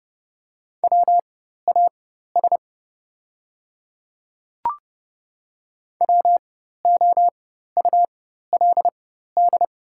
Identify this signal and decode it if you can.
Morse code, heard as a keyed tone, about 30 wpm, spaced as for 14 wpm, 700 Hz. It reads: WAS WOULD
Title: 4x Character Spacing - Top 300 Words - 30wpm